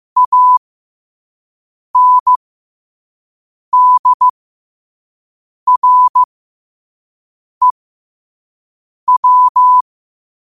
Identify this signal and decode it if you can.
Morse code, heard as a keyed tone, about 15 wpm, spaced as for 5 wpm, 1000 Hz. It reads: ANDREW
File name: qotdfiles / 5wpm